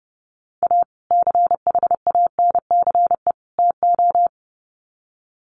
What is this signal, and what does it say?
Morse code, heard as a keyed tone, about 30 wpm, 700 Hz. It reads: A CHANCE TO
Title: Sets of 3 Words 30wpm